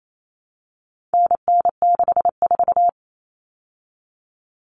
Morse code, heard as a keyed tone, about 28 wpm, 700 Hz. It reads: NN64
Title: Common Cut Numbers - 4-Digit 28wpm